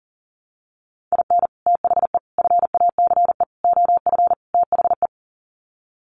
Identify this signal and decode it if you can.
Morse code, heard as a keyed tone, about 40 wpm, 700 Hz. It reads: IN THE FACE OF THE
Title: Sets of 5 Words 40wpm